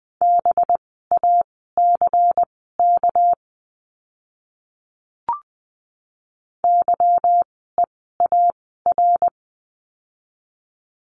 Morse code, heard as a keyed tone, about 20 wpm, spaced as for 15 wpm, 700 Hz. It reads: BACK YEAR
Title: 244 Lesson 39f - Letter z - Words 20wpm